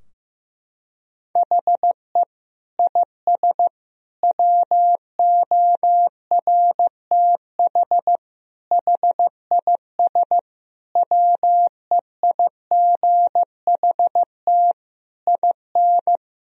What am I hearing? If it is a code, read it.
Morse code, heard as a keyed tone, about 15 wpm, 700 Hz. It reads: HE IS WORTH HIS WEIGHT IN
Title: Sentences from Top 1000 Words - Review Part 3 - 15wpm